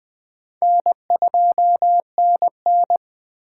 Morse code, heard as a keyed tone, about 20 wpm, 700 Hz. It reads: N2NN
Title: Speed Racing - USA Call Signs 17wpm